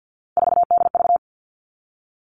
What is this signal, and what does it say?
Morse code, heard as a keyed tone, about 50 wpm, 700 Hz. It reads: 4DV
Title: Three Challenging Letter-Number Combinations 50wpm